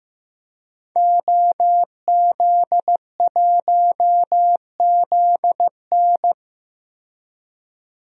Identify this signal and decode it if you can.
Morse code, heard as a keyed tone, about 15 wpm, 700 Hz. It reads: OZ1ZN
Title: DXCC-100 Call Signs - 5 Characters - 15wpm